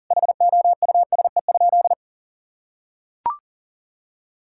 Morse code, heard as a keyed tone, about 40 wpm, 700 Hz. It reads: HOUSE?
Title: Sentences from Top 1000 Words - Review Part 6 - 40wpm